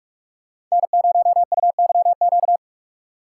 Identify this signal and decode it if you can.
Morse code, heard as a keyed tone, about 42 wpm, 690 Hz. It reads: N0UYQ
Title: USA Call Signs - 5 Characters - 2x Character Spacing - 45wpm